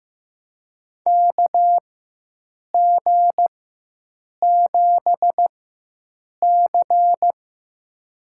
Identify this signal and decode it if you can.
Morse code, heard as a keyed tone, about 15 wpm, spaced as for 7 wpm, 700 Hz. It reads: KG7C
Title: USA Call Signs - 6 Characters - 4x Character Spacing - 15wpm